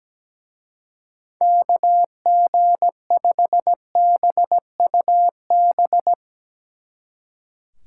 Morse code, heard as a keyed tone, about 18 wpm, 700 Hz. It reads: KG5BUB
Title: Speed Racing - USA Call Signs 17wpm